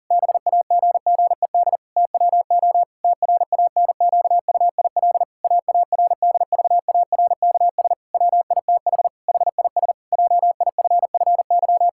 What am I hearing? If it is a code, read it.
Morse code, heard as a keyed tone, about 40 wpm, 700 Hz. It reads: BAGGED TWO TRANQUIL AARDVARKS WITH HIS JIFFY
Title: Pangram Challenge v2 - 40wpm